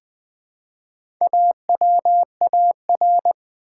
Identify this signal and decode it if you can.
Morse code, heard as a keyed tone, about 20 wpm, 700 Hz. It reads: AWAR
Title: Build A Word - 1 Letter At A Time - 3 to 5-Letter Words - 20wpm